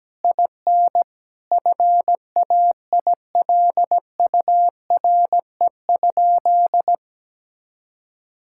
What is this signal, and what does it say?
Morse code, heard as a keyed tone, about 17 wpm, 700 Hz. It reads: IN FAILURE?
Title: Sentences from Top 1000 Words - Review Part 6 - 17wpm